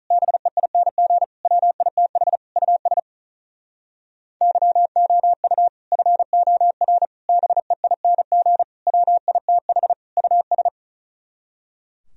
Morse code, heard as a keyed tone, about 47 wpm, 700 Hz. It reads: BEING WITH US YOU FOR BEING WITH US